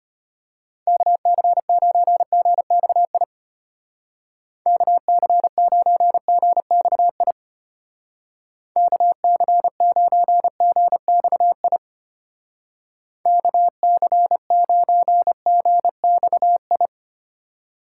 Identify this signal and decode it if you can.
Morse code, heard as a keyed tone, about 30 wpm, 700 Hz. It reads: KC9GXI KC9GXI KC9GXI KC9GXI
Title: Speed Racing - USA Call Signs 25wpm